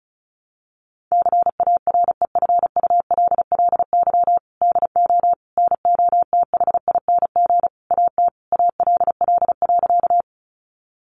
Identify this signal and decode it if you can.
Morse code, heard as a keyed tone, about 34 wpm, 700 Hz. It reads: CAREFULLY DO NOTHING AT ALL.
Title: Sentences from Top 1300 Words 35wpm